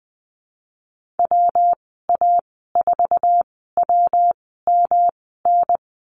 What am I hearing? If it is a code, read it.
Morse code, heard as a keyed tone, about 20 wpm, spaced as for 15 wpm, 700 Hz. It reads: WA4WMN
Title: 173 Lesson 28g - Number 9 - Call Signs 20wpm